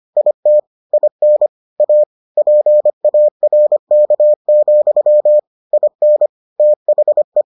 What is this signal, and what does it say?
Morse code, heard as a keyed tone, about 25 wpm, 600 Hz. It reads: IT IN A PARK, IN THE